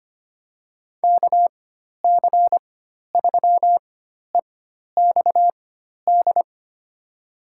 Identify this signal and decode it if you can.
Morse code, heard as a keyed tone, about 25 wpm, spaced as for 12 wpm, 700 Hz. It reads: KC3EXD
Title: USA Call Signs - 4x Character Spacing - 25wpm